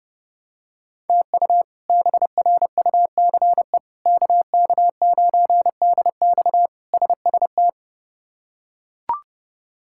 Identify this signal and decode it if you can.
Morse code, heard as a keyed tone, about 30 wpm, 700 Hz. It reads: TU BRUCE KK9DX SST
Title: K1USN SST QSO Practice - 30wpm